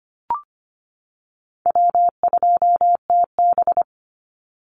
Morse code, heard as a keyed tone, about 25 wpm, 700 Hz. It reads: W2TB
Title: Easy Call Signs - 25wpm